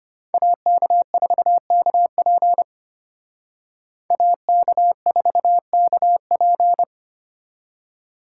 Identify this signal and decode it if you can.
Morse code, heard as a keyed tone, about 32 wpm, 700 Hz. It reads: AK4KP AK4KP